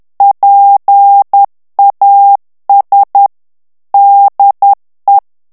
Morse code, heard as a keyed tone, about 10 wpm, 800 Hz. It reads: PAS DE
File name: morse mp3 / FRANCAIS TEXTE CLAIR 2 VITESSE 10 MOTS MINUTES